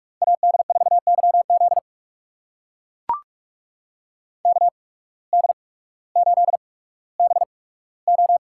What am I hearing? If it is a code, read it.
Morse code, heard as a keyed tone, about 44 wpm, spaced as for 19 wpm, 700 Hz. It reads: AD4YZ KD8BO
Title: USA Call Signs - 5 Characters - 8x Character Spacing - 45wpm